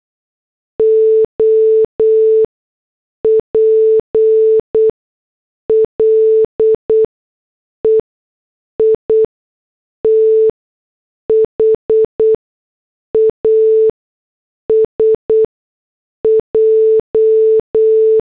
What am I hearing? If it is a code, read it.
Morse code, heard as a keyed tone, about 8 wpm, spaced as for 6 wpm, 440 Hz. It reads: OPLE IT HAS J